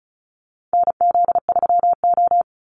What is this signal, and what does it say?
Morse code, heard as a keyed tone, about 34 wpm, 700 Hz. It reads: NZ3O
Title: Call Signs Repeated 2x - 35wpm